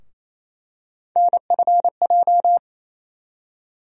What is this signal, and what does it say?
Morse code, heard as a keyed tone, about 28 wpm, 700 Hz. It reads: NFJ
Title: USA Call Sign Postfixes - 3 Letters 28wpm